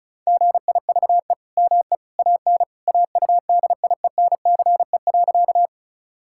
Morse code, heard as a keyed tone, about 36 wpm, 700 Hz. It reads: GIVE ME AN AUDIENCE.
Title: Sentences from Top 1000 Words - Review Part 5 - 35wpm